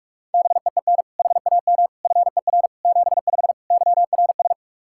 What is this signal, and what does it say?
Morse code, heard as a keyed tone, about 46 wpm, 700 Hz. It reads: BEEN HAM FER 75 YRS
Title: Practice QSO - Exchange 3 - More Info - 45wpm